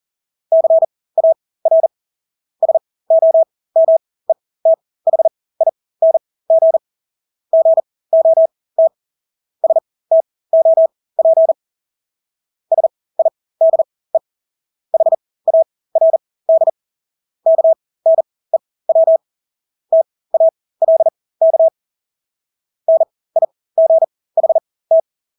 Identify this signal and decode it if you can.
Morse code, heard as a keyed tone, about 40 wpm, spaced as for 20 wpm, 650 Hz. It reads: CAR SOMETHING GOT STOP SIDE HARD KNEW TALK NIGHT
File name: Engelske ord 40-20 wpm